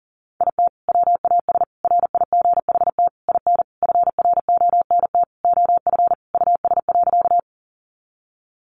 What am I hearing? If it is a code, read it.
Morse code, heard as a keyed tone, about 40 wpm, 700 Hz. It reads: IT WAS RIGHT IN FRONT OF US.
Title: Sentences from Top 500 Words 40wpm